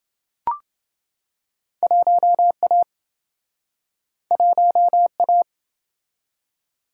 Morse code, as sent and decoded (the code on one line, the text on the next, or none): .---- .- / .---- .-
1A 1A